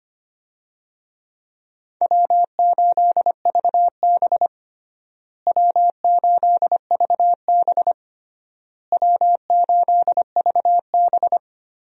Morse code, heard as a keyed tone, about 25 wpm, 700 Hz. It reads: W8VB W8VB W8VB
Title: USA Call Signs - 4 Characters - Repeated 3x - 25wpm